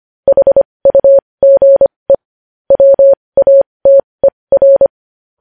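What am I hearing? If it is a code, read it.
Morse code, heard as a keyed tone, about 25 wpm, spaced as for 20 wpm, 550 Hz. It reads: HUGE WATER
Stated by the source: Morse practice sending news headlines at 20 wpm.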